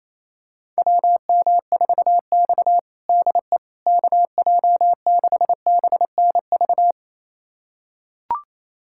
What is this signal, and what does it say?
Morse code, heard as a keyed tone, about 27 wpm, 700 Hz. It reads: WM4X DE KJ6BNV
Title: USA Callsign DE Callsign 28wpm